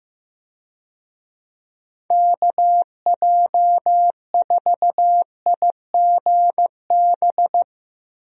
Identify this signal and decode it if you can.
Morse code, heard as a keyed tone, about 15 wpm, 700 Hz. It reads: KJ4IGB
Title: USA Call Signs - 6 Characters - Repeated 3x - 15wpm